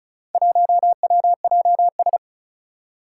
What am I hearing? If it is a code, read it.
Morse code, heard as a keyed tone, about 36 wpm, 700 Hz. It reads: 1WJS
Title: Four Challenging Letter-Number Combinations 35wpm